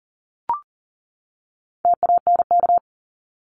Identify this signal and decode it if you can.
Morse code, heard as a keyed tone, about 40 wpm, 700 Hz. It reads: TANK
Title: Top 1751-2000 Words 40wpm